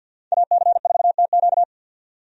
Morse code, heard as a keyed tone, about 49 wpm, 700 Hz. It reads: AC4TQ
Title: USA Call Signs - 5 Characters - 50wpm